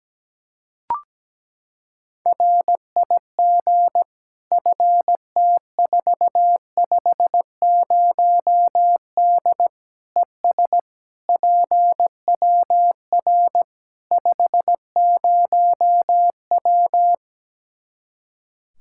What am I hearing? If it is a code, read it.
Morse code, heard as a keyed tone, about 17 wpm, 700 Hz. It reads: RIG FT450D ES PWR 50W
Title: K1USN SST QSO Practice 17wpm